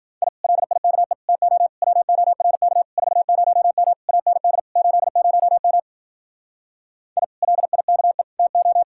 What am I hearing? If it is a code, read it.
Morse code, heard as a keyed tone, about 51 wpm, 700 Hz. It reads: I LIKE TO WORK 40M AND 80M I LIKE TO